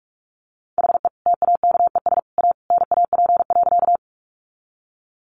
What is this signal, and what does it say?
Morse code, heard as a keyed tone, about 44 wpm, 700 Hz. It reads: HE TAKES A NAP.
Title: Easy Phrases 45wpm